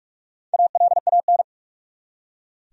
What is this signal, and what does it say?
Morse code, heard as a keyed tone, about 46 wpm, 700 Hz. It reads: ALAN
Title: USA Common Names - Max 5 Letters - 45wpm